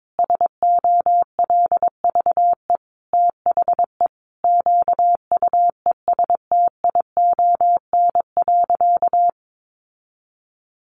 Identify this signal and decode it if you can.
Morse code, heard as a keyed tone, about 22 wpm, 700 Hz. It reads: SOLVE THE QUESTION.
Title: Sentences from Top 1000 Words 22wpm